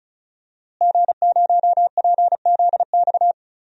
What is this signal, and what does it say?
Morse code, heard as a keyed tone, about 36 wpm, 700 Hz. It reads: G0PZX
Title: DXCC-100 Call Signs - 5 Characters - Repeated 3x - 35wpm